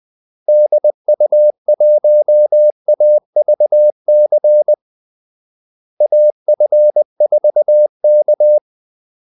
Morse code, heard as a keyed tone, about 20 wpm, 600 Hz. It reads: DU1AVC AF4K
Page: Kaldesignaler 20 wpm